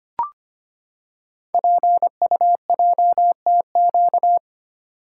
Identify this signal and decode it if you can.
Morse code, heard as a keyed tone, about 25 wpm, 700 Hz. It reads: PUJTQ
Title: Code Groups - 5 Letter:Numbers - 25wpm